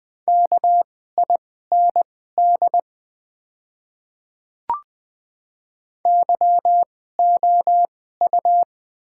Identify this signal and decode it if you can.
Morse code, heard as a keyed tone, about 20 wpm, spaced as for 14 wpm, 700 Hz. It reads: KIND YOU
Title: 2x Character Spacing - Top 500 Words - 20wpm